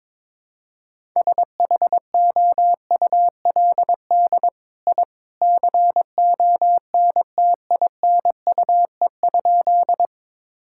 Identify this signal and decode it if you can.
Morse code, heard as a keyed tone, about 22 wpm, 700 Hz. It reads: SHOULD I CONTINUE?
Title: Sentences from Top 1000 Words - Review Part 6 - 22wpm